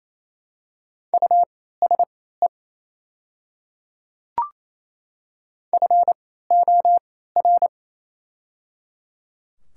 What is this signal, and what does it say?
Morse code, heard as a keyed tone, about 28 wpm, spaced as for 16 wpm, 700 Hz. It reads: USE FOR
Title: 3x Character Spacing - Top 100 Words - 28wpm